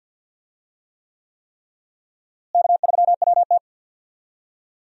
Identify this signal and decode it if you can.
Morse code, heard as a keyed tone, about 52 wpm, 700 Hz. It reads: K3WT
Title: USA Call Signs - 4 Characters - Repeated 3x - 50wpm